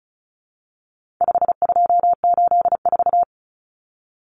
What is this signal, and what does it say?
Morse code, heard as a keyed tone, about 34 wpm, 700 Hz. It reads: 5284
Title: Common Cut Numbers - 4-Digit 35wpm